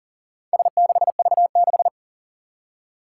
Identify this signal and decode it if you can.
Morse code, heard as a keyed tone, about 37 wpm, 700 Hz. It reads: S6V6
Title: Four Challenging Letter-Number Combinations 40wpm